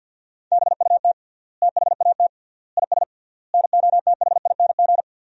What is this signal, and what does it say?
Morse code, heard as a keyed tone, about 50 wpm, 700 Hz. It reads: BUT THAT IS NOTHING
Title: 3x Word-Spacing - Sentences from Top 500 Words - 50wpm